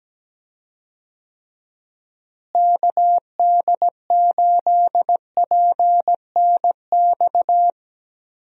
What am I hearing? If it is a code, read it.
Morse code, heard as a keyed tone, about 17 wpm, 700 Hz. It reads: KD8PNX
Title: USA Call Signs - 6 Characters - Repeated 3x - 17wpm